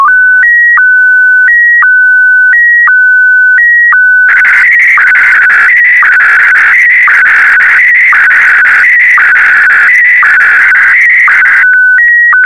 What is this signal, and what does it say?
SSTV (slow-scan television), Scottie DX.